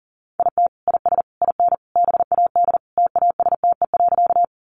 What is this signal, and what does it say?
Morse code, heard as a keyed tone, about 40 wpm, 700 Hz. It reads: IT IS IN BAD TASTE.
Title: Sentences from Top 1100 Words 40wpm